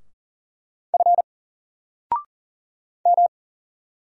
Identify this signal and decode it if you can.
Morse code, heard as a keyed tone, about 40 wpm, 700 Hz. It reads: F M
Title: Single Letter-Number 40wpm